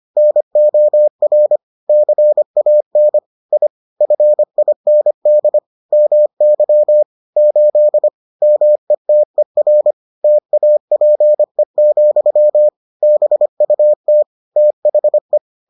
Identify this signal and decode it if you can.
Morse code, heard as a keyed tone, about 25 wpm, 600 Hz. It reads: NOR CAN I FIND MY 8 METER TAPE, BUT THE